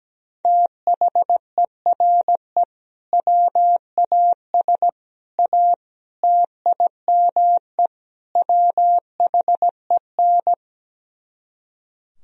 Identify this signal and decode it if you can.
Morse code, heard as a keyed tone, about 17 wpm, 700 Hz. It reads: THERE WAS A TIME WHEN
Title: Speed-Racing - Sets of 5 Words 17wpm